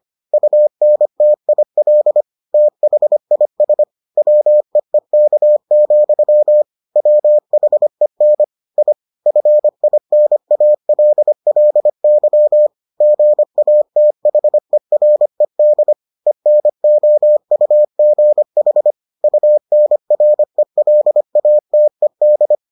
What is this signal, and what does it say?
Morse code, heard as a keyed tone, about 25 wpm, 600 Hz. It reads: UNTIL THIS WEEK, WHEN I FINALLY GATHERED ENOUGH UNRELATED